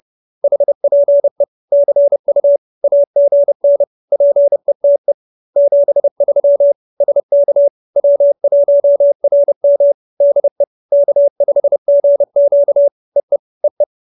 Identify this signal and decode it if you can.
Morse code, heard as a keyed tone, about 30 wpm, 570 Hz. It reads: HPE CU AGN PETE 73 SK W1RM DE K5GQ EE EE
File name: QSO308 30